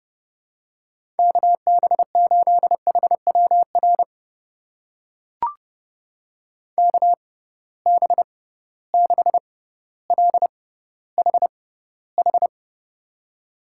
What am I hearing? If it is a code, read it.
Morse code, heard as a keyed tone, about 30 wpm, spaced as for 15 wpm, 700 Hz. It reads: KB8HWR KB6LHH
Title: USA Call Signs - 6 Characters - 6x Character Spacing - 30wpm